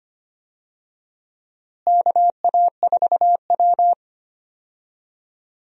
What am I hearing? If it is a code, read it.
Morse code, heard as a keyed tone, about 25 wpm, 700 Hz. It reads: KA4W
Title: USA Call Signs - 4 Characters - Repeated 3x - 25wpm